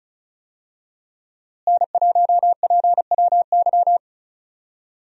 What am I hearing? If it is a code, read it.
Morse code, heard as a keyed tone, about 36 wpm, 700 Hz. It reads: N1PWY
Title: USA Call Signs - 5 Characters - Repeated 2x - 35wpm